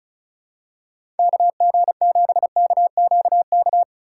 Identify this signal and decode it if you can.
Morse code, heard as a keyed tone, about 36 wpm, 700 Hz. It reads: KG7KQK
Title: USA Call Signs - 6 Characters - 4x Character Spacing - 35wpm